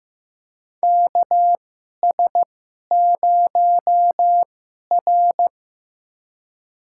Morse code, heard as a keyed tone, about 15 wpm, spaced as for 11 wpm, 700 Hz. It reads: KS0R
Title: USA Call Signs - 4 Characters - 2x Character Spacing - 15wpm